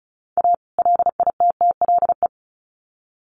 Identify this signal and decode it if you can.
Morse code, heard as a keyed tone, about 34 wpm, 700 Hz. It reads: A LITTLE
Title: Sets of 2 Words 35wpm